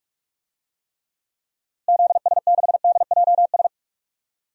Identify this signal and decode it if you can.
Morse code, heard as a keyed tone, about 41 wpm, 700 Hz. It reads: ZS6DJS
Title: DXCC-100 Call Signs - 6 Characters - Repeated 2x - 45wpm